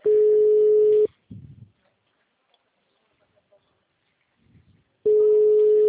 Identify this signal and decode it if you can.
Ringback tone (Europe, 425 Hz).
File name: out- -2005- - - .27407